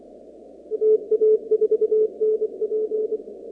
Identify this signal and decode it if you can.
Morse code, heard as a keyed tone, about 23 wpm, 430 Hz. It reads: TA4NP
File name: contests / 2019-cw-wpx-kt5j-k5pi- -2330 0-37